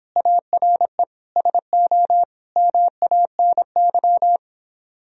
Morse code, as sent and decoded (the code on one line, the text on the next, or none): .- .-. / ... --- / -- .- -. -.--
ARE SO MANY